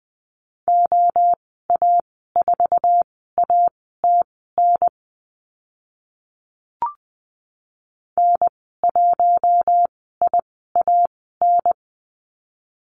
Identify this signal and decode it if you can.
Morse code, heard as a keyed tone, about 20 wpm, spaced as for 14 wpm, 700 Hz. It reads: OA4ATN N1IAN
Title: 062 Lesson 11g - Letter h - Call Signs 20wpm